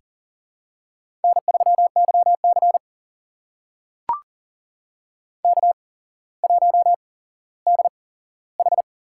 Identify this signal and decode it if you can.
Morse code, heard as a keyed tone, about 40 wpm, spaced as for 17 wpm, 700 Hz. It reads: N3YC K1DH
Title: USA Call Signs - 4 Characters - 8x Character Spacing - 40wpm